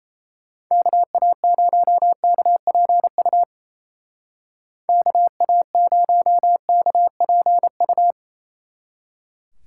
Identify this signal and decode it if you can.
Morse code, heard as a keyed tone, about 36 wpm, 700 Hz. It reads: KA0KPU KA0KPU